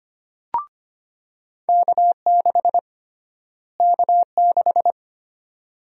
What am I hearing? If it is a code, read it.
Morse code, heard as a keyed tone, about 25 wpm, 700 Hz. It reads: K6 K6
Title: USA Call Sign Prefixes - Repeated 2x - 25wpm